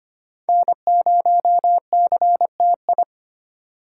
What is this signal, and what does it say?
Morse code, heard as a keyed tone, about 34 wpm, spaced as for 30 wpm, 700 Hz. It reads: N0CTI
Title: Speed Racing - USA 1x3 Call Signs 25wpm